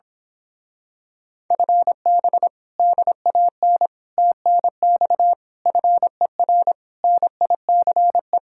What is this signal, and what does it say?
Morse code, heard as a keyed tone, about 26 wpm, 700 Hz. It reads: FB DAN TNX FER NICE
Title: Speed-Racing Common QSO Phrases - 22wpm